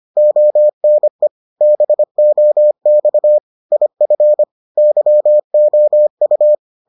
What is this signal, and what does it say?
Morse code, heard as a keyed tone, about 25 wpm, 600 Hz. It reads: ONE BOX IF YOU